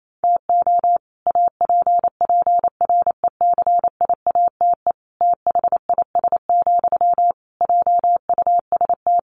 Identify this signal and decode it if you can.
Morse code, heard as a keyed tone, about 28 wpm, 700 Hz. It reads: TO APPRECIATE THIS, JUST
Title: Sentences from Top 1300 Words 28wpm